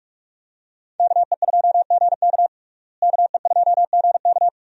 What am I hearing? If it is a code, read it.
Morse code, heard as a keyed tone, about 44 wpm, 700 Hz. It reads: KE2GK KE2GK
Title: USA Call Signs - 5 Characters - Repeated 3x - 45wpm